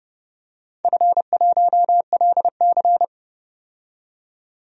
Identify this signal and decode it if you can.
Morse code, heard as a keyed tone, about 30 wpm, 700 Hz. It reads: F1LC
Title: DXCC-100 Call Signs - 4 Characters - 30wpm